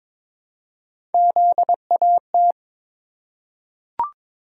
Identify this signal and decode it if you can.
Morse code, heard as a keyed tone, about 21 wpm, 700 Hz. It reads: ZAT
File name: practice / 22wpm / Three Letter-Number Combinations - 22wpm